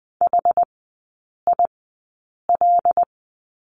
Morse code, heard as a keyed tone, about 20 wpm, spaced as for 8 wpm, 700 Hz. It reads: HIL
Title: 074 Lesson 13e - Letter L - Three Characters 20wpm